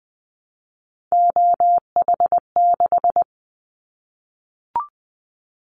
Morse code, heard as a keyed tone, about 20 wpm, 700 Hz. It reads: OH6
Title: Callsign Prefixes for Top 25 DXCC - 20wpm